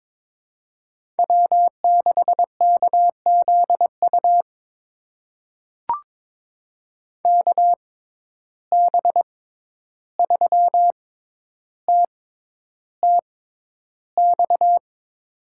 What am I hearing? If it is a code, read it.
Morse code, heard as a keyed tone, about 21 wpm, spaced as for 11 wpm, 700 Hz. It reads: W6KZU KB3TTX
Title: USA Call Signs - 6x Character Spacing - 22wpm